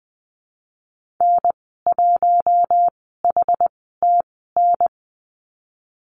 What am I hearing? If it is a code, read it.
Morse code, heard as a keyed tone, about 20 wpm, spaced as for 15 wpm, 700 Hz. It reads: N1HTN
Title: 088 Lesson 15g - Number 5 - Call Signs 20wpm